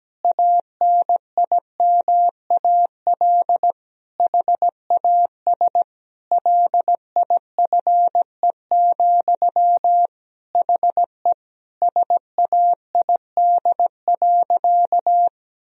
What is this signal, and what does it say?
Morse code, heard as a keyed tone, about 17 wpm, 700 Hz. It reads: ANIMAL HAS LIFE, HE SAID.
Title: Sentences from Top 1000 Words - Review Part 1 - 17wpm